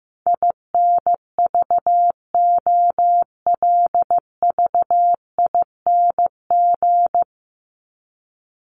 Morse code, heard as a keyed tone, about 15 wpm, 700 Hz. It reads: INVOLVING
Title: ING Suffix - Top 100 Words with Base Words 15wpm